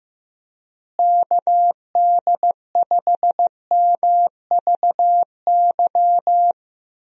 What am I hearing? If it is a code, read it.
Morse code, heard as a keyed tone, about 15 wpm, 700 Hz. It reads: KD5MVY
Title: USA Call Signs - 6 Characters - Repeated 2x - 15wpm